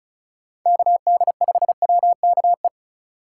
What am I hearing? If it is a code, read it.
Morse code, heard as a keyed tone, about 34 wpm, 700 Hz. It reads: KD5WKE
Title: USA Call Signs - 2x Character Spacing - 35wpm